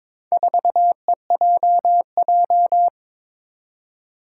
Morse code, heard as a keyed tone, about 21 wpm, 700 Hz. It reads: 4EJJ
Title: Four Challenging Letter-Number Combinations 22wpm